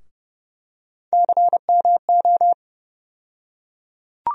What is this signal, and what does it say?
Morse code, heard as a keyed tone, about 29 wpm, 700 Hz. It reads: CMO
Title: USA Call Sign Postfixes - 3 Letters 30wpm